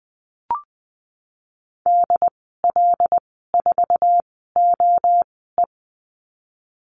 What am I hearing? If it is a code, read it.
Morse code, heard as a keyed tone, about 19 wpm, spaced as for 14 wpm, 700 Hz. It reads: DL4OE
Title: 088 Lesson 15g - Number 5 - Call Signs 20wpm